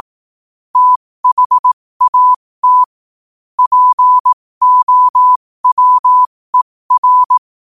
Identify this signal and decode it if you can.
Morse code, heard as a keyed tone, about 18 wpm, spaced as for 15 wpm, 1000 Hz. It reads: THAT POWER
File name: qotdfiles / 15wpm